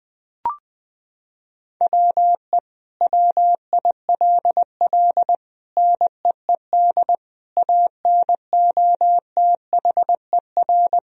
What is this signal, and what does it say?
Morse code, heard as a keyed tone, about 20 wpm, 700 Hz. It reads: WE WILL NEED ANOTHER
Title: Sentences from Top 1000 Words - Review Part 4 - 20wpm